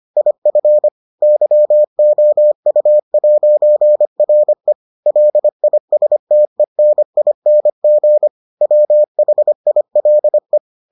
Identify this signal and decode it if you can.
Morse code, heard as a keyed tone, about 25 wpm, 600 Hz. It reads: IF YOU'RE LISTENING WHILE